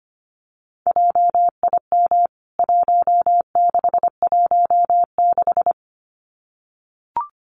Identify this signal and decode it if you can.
Morse code, heard as a keyed tone, about 25 wpm, 700 Hz. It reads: JIM 1616
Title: CWT Exchange 25wpm